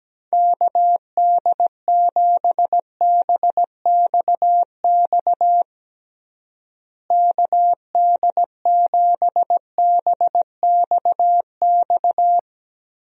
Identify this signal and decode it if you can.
Morse code, heard as a keyed tone, about 17 wpm, 700 Hz. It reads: KD7BXX KD7BXX
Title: USA Call Signs - 6 Characters - Repeated 3x - 17wpm